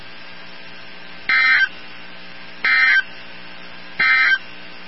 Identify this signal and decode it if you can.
Emergency Alert System (SAME) tones, heard as harsh buzzing bursts. It is heard from 1.2 to 4.5 s.
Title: NOAA Weather Radio 162.550